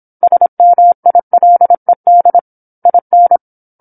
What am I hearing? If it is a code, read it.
Morse code, heard as a keyed tone, about 27 wpm, 700 Hz. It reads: SMILED IN
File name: 08-Chapter-08 26 WPM